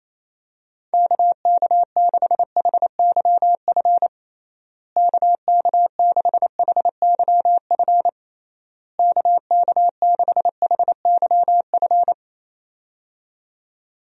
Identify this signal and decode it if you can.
Morse code, heard as a keyed tone, about 28 wpm, 700 Hz. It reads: KK6HYF KK6HYF KK6HYF
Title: USA Call Signs - 6 Characters - Repeated 3x - 28wpm